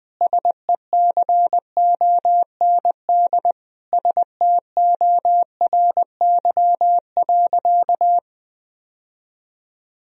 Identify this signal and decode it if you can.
Morse code, heard as a keyed tone, about 21 wpm, 700 Hz. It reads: SECOND STORY.
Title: Sentences from Top 1000 Words - Review Part 1 - 20wpm